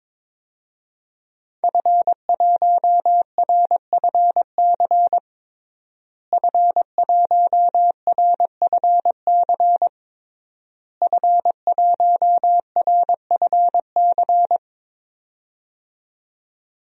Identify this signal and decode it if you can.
Morse code, heard as a keyed tone, about 22 wpm, 700 Hz. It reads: F1RFC F1RFC F1RFC
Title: DXCC-100 Call Signs - 5 Characters - Repeated 3x - 22wpm